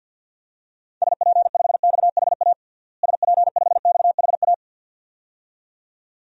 Morse code, heard as a keyed tone, about 48 wpm, 700 Hz. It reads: SP5XHA SP5XHA
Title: DXCC-100 Call Signs - 6 Characters - Repeated 2x - 50wpm